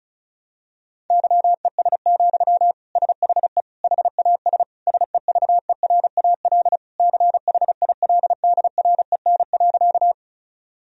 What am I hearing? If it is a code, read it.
Morse code, heard as a keyed tone, about 34 wpm, 700 Hz. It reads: YES, SHE HAS SEVERAL CHILDREN.
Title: Sentences from Top 1000 Words - Review Part 2 - 35wpm